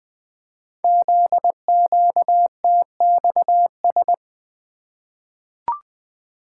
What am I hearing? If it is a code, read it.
Morse code, heard as a keyed tone, about 20 wpm, 700 Hz. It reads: ZQTXS
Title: Code Groups - 5 Letters - 20wpm